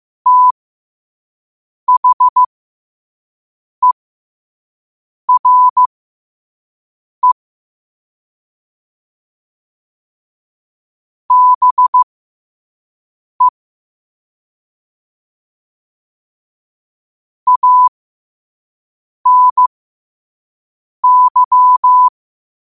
Morse code, heard as a keyed tone, about 15 wpm, spaced as for 5 wpm, 1000 Hz. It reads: THERE BE ANY
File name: qotdfiles / 5wpm